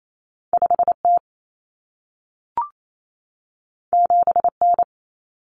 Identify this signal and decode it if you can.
Morse code, heard as a keyed tone, about 28 wpm, 700 Hz. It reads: 5T 7N
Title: Common Cut Numbers - 2-Digit 28wpm